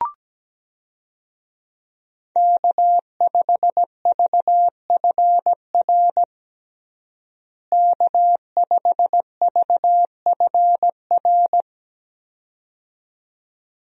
Morse code, heard as a keyed tone, about 17 wpm, 700 Hz. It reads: K5VFR K5VFR
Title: USA Call Signs - 5 Characters - Repeated 2x - 17wpm